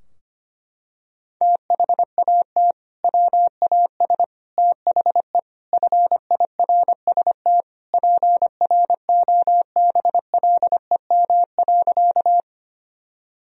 Morse code, heard as a keyed tone, about 25 wpm, 700 Hz. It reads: THAT WAS THE FIRST PROBLEM.
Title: Sentences from Top 1000 Words - Review Part 2 - 25wpm